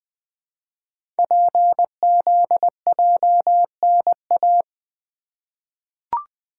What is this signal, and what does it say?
Morse code, heard as a keyed tone, about 20 wpm, 700 Hz. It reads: PZJNA
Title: Code Groups - 5 Letters - 20wpm